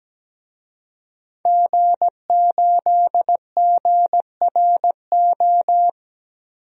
Morse code, heard as a keyed tone, about 17 wpm, 700 Hz. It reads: G8GRO
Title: DXCC-100 Call Signs - 5 Characters - 17wpm